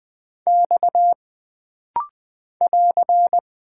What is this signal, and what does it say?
Morse code, heard as a keyed tone, about 20 wpm, 700 Hz. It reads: X <AR>
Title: Single Character (Letter:Number:Punctuation:Prosign) - Rapid-Fire - 20wpm